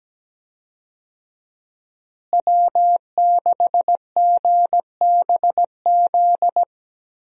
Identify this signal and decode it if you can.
Morse code, heard as a keyed tone, about 17 wpm, 700 Hz. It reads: W6GBZ
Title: USA Call Signs - 5 Characters - Repeated 2x - 17wpm